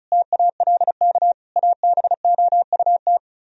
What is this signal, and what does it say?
Morse code, heard as a keyed tone, about 47 wpm, spaced as for 41 wpm, 700 Hz. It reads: TALK ABOUT
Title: Speed-Racing - Sets of 5 Words 35wpm